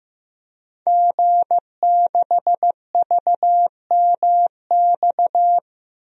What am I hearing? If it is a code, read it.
Morse code, heard as a keyed tone, about 15 wpm, 700 Hz. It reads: G6VMX
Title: DXCC-100 Call Signs - 5 Characters - Repeated 2x - 15wpm